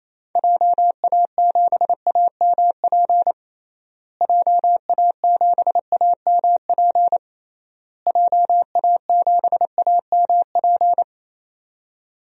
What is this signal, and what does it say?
Morse code, heard as a keyed tone, about 28 wpm, 700 Hz. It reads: JA7AMP JA7AMP JA7AMP
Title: DXCC-100 Call Signs - 6 Characters - Repeated 3x - 28wpm